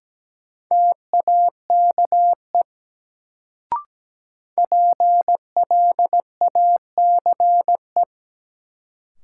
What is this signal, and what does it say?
Morse code, heard as a keyed tone, about 17 wpm, 700 Hz. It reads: TAKE PLACE
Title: Sentences from Top 200 Words - New Word Drill - 17wpm